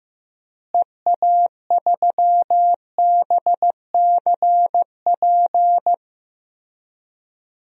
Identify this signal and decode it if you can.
Morse code, heard as a keyed tone, about 15 wpm, 700 Hz. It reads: EA3BCP
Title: DXCC-100 Call Signs - 6 Characters - Repeated 3x - 15wpm